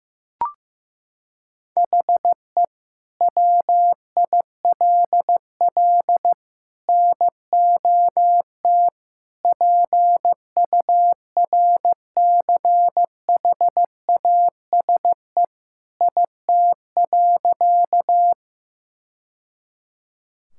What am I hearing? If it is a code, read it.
Morse code, heard as a keyed tone, about 15 wpm, 700 Hz. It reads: HE WILL NOT PURCHASE IT.
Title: Sentences from Top 1000 Words - Review Part 5 - 15wpm